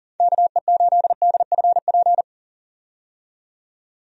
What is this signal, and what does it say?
Morse code, heard as a keyed tone, about 40 wpm, 700 Hz. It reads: KE8DFP
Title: USA Call Signs - 6 Characters - 40wpm